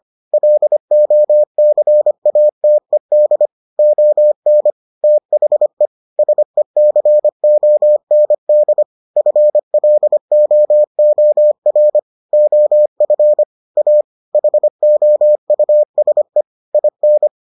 Morse code, heard as a keyed tone, about 25 wpm, 600 Hz. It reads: LOCATED ON THE SECOND FLOOR OF A HOUSE IN